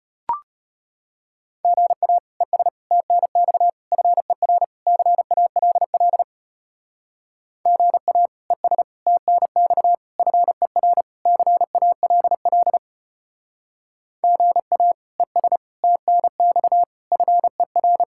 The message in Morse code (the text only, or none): GA ES TNX FER CALL GA ES TNX FER CALL GA ES TNX FER